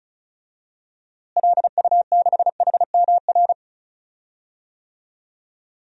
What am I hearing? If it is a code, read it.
Morse code, heard as a keyed tone, about 36 wpm, 700 Hz. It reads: LU6HMR
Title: DXCC-100 Call Signs - 6 Characters - Repeated 3x - 35wpm